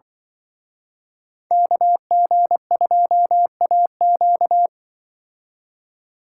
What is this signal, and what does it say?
Morse code, heard as a keyed tone, about 23 wpm, 700 Hz. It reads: KG2AQ
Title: Speed Racing - USA Call Signs 20wpm